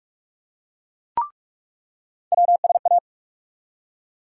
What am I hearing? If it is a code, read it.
Morse code, heard as a keyed tone, about 46 wpm, 700 Hz. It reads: WSA
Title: USA Call Sign Postfixes - 3 Letters 45wpm